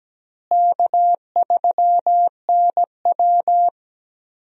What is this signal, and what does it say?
Morse code, heard as a keyed tone, about 17 wpm, 700 Hz. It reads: K3NW
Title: USA Call Signs - 4 Characters - Repeated 2x - 17wpm